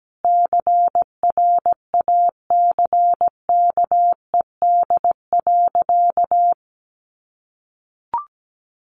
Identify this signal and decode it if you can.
Morse code, heard as a keyed tone, about 17 wpm, 700 Hz. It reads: CRACKED.
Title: A Sound Of Thunder 17wpm